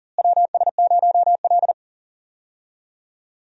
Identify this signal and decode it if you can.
Morse code, heard as a keyed tone, about 40 wpm, 700 Hz. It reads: WS0L
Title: USA Call Signs - 4 Characters - Repeated 3x - 40wpm